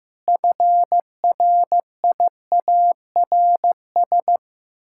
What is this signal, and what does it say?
Morse code, heard as a keyed tone, about 15 wpm, 700 Hz. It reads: FRIARS
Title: Pangram Challenge v2 - 15wpm